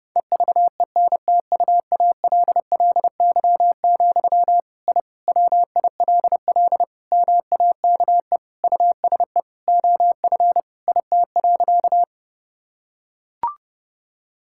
Morse code, heard as a keyed tone, about 30 wpm, 700 Hz. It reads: EVENTUALLY, I WILL MAKE USE OF IT.
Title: Sentences from Top 1000 Words - Review Part 6 - 30wpm